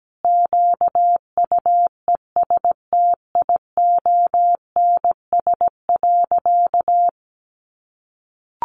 Morse code, heard as a keyed tone, about 17 wpm, 700 Hz. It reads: QUESTIONS.
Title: Additional Easy Phrases 17wpm